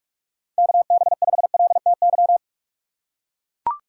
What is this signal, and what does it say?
Morse code, heard as a keyed tone, about 41 wpm, 700 Hz. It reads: KB5LTY
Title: USA Call Signs - 6 Characters - 2x Character Spacing - 45wpm